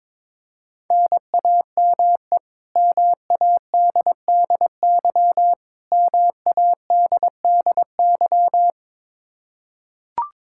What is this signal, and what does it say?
Morse code, heard as a keyed tone, about 21 wpm, 700 Hz. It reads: NAME MADDY MADDY
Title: Practice QSO - Exchange 1 - Beginning - 22wpm